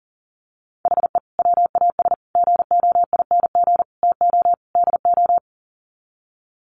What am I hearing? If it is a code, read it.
Morse code, heard as a keyed tone, about 40 wpm, 700 Hz. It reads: HE WAS GOING TO DO
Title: Sets of 5 Words Encore2 40wpm